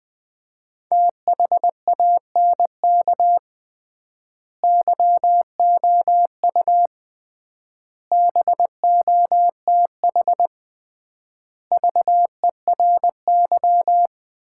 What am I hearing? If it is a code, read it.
Morse code, heard as a keyed tone, about 20 wpm, 700 Hz. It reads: THANK YOU BOTH VERY
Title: 3x Word-Spacing - Sets of 5 Words - 20wpm